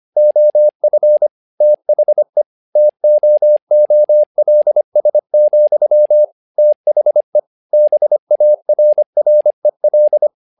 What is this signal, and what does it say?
Morse code, heard as a keyed tone, about 25 wpm, 600 Hz. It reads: OF THE TOOLS, THE BARREL
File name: learning center / tower trouble / tower trouble 25wpm